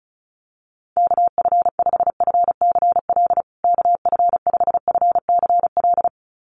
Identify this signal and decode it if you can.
Morse code, heard as a keyed tone, about 34 wpm, 700 Hz. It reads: KF5FCL KF5FCL
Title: USA Call Signs Repeated 2x 35wpm